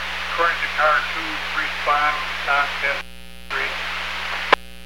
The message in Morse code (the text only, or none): none